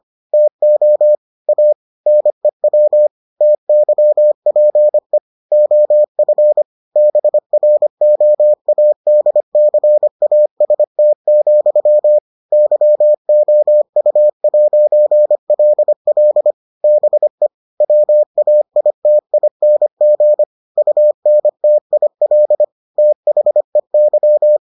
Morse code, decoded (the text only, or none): TO A NEW TYPE OF BROADCAST, YOU'LL BE WAITING UNTIL THEY